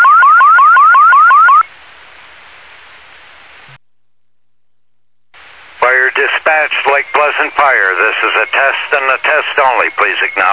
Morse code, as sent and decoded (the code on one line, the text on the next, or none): none